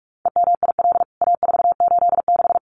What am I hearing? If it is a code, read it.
Morse code, heard as a keyed tone, about 44 wpm, 700 Hz. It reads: EMIL A486
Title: CWT Exchange 45wpm